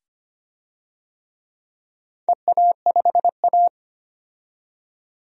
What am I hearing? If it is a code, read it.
Morse code, heard as a keyed tone, about 25 wpm, 700 Hz. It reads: EA5A
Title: DXCC-100 Call Signs - 4 Characters - 25wpm